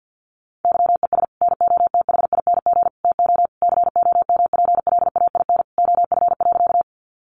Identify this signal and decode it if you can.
Morse code, heard as a keyed tone, about 50 wpm, 700 Hz. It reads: YES NOTHING TO COMPLAIN OF.
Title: Sentences from Top 1300 Words 50wpm